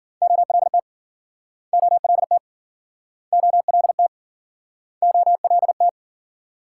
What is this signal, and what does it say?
Morse code, heard as a keyed tone, about 53 wpm, 700 Hz. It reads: OLT OLT OLT OLT